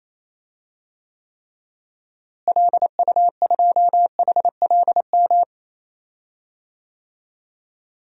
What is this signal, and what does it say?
Morse code, heard as a keyed tone, about 28 wpm, 700 Hz. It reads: LU2HLM
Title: DXCC-100 Call Signs - 6 Characters - Repeated 3x - 28wpm